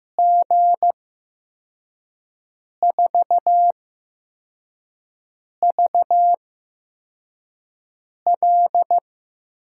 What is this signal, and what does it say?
Morse code, heard as a keyed tone, about 15 wpm, spaced as for 4 wpm, 700 Hz. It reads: G4VL
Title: USA Call Signs - 6 Characters - 8x Character Spacing - 15wpm